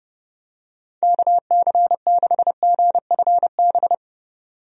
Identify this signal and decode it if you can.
Morse code, heard as a keyed tone, about 29 wpm, 700 Hz. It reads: KC6GFB
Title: USA Call Signs - 6 Characters - 30wpm